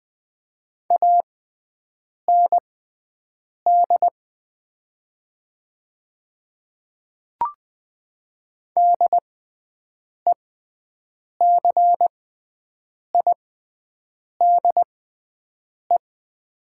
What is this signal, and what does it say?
Morse code, heard as a keyed tone, about 20 wpm, spaced as for 7 wpm, 700 Hz. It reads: AND DECIDE
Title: 6x Character Spacing - Top 500 Words - 20wpm